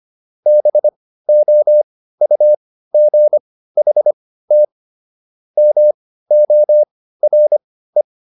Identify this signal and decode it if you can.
Morse code, heard as a keyed tone, about 25 wpm, spaced as for 15 wpm, 600 Hz. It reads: BOUGHT MORE